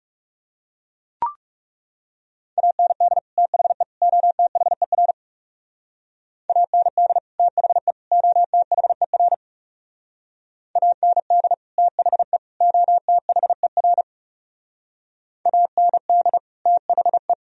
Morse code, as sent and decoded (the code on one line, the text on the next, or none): .- -. -.. / - .... / --- - .... .-. / .- -. -.. / - .... / --- - .... .-. / .- -. -.. / - .... / --- - .... .-. / .- -. -.. / - .... .
AND THE OTHER AND THE OTHER AND THE OTHER AND THE